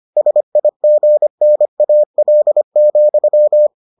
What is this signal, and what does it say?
Morse code, heard as a keyed tone, about 25 wpm, 600 Hz. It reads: SIGNAL,